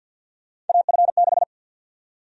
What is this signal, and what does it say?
Morse code, heard as a keyed tone, about 53 wpm, 700 Hz. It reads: AF6
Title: USA Call Sign Prefixes - Repeated 2x - 50wpm